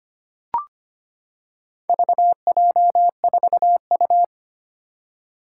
Morse code, heard as a keyed tone, about 25 wpm, 700 Hz. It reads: VJ4U
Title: Four Challenging Letter-Number Combinations 25wpm